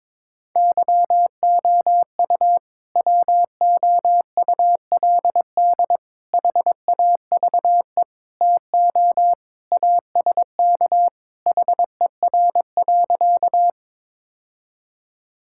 Morse code, heard as a keyed tone, about 22 wpm, 700 Hz. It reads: YOU WOULD HAVE TO ASK HER.
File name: practice / 22wpm / Sentences with 3-Word Phrases - 22wpm